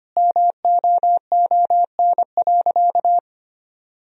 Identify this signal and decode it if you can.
Morse code, heard as a keyed tone, about 25 wpm, 700 Hz. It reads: MOON.
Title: Sentences from Top 1000 Words - Review Part 3 - 25wpm